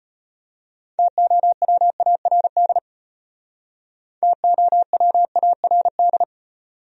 Morse code, heard as a keyed tone, about 33 wpm, 700 Hz. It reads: TOWARD TOWARD